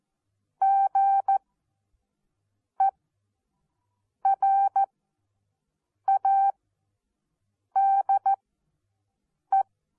Morse code, heard as a keyed tone, about 14 wpm, spaced as for 6 wpm, 780 Hz. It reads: GERADE
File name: Onlinelehrgang / morsen / DJ4UF Morsen14